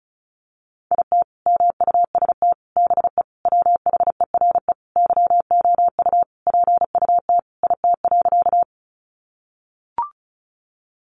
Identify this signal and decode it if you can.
Morse code, heard as a keyed tone, about 34 wpm, 700 Hz. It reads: IT MUST BE WHERE YOU PUT IT.
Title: Sentences from Top 200 Words 35wpm